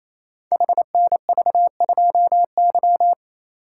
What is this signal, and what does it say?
Morse code, heard as a keyed tone, about 27 wpm, 700 Hz. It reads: HNV2Y
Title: Code Groups - 5 Letter:Numbers - 28wpm